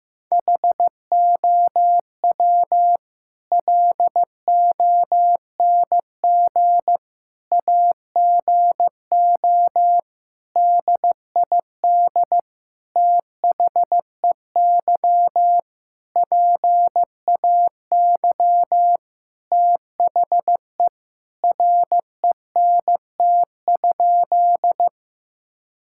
Morse code, heard as a keyed tone, about 15 wpm, 700 Hz. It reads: HOW LONG AGO DID THEY PAY THE RENT?
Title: Sentences from Top 1000 Words - Review Part 5 - 15wpm